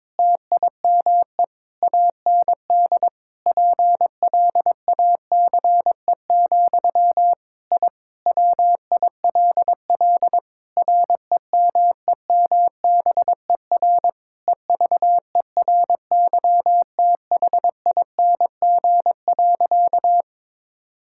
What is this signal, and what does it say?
Morse code, heard as a keyed tone, about 22 wpm, 700 Hz. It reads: TIME AND PLACE, I WILL REMEMBER EVERYTHING.
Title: Sentences from Top 1000 Words - Review Part 5 - 22wpm